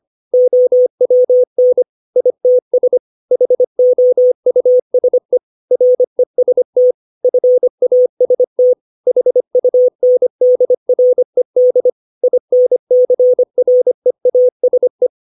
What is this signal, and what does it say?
Morse code, heard as a keyed tone, about 25 wpm, 500 Hz. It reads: OWN ITS HOUSE REST FAST HUNDRED INCREASE
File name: 632 words / 632 Words 25WPM